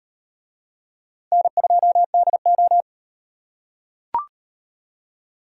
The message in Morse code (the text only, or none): N2DO